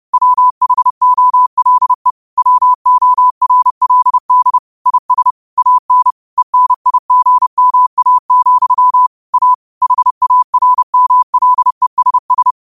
Morse code, heard as a keyed tone, about 30 wpm, 1000 Hz. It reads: WHOLE WORLD IS AN ENIGMA, A HARMLESS
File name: qotdfiles / 30wpm